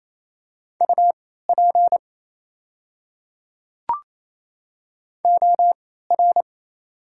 Morse code, heard as a keyed tone, about 28 wpm, spaced as for 16 wpm, 700 Hz. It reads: UP OR
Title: 3x Character Spacing - Top 200 Words - 28wpm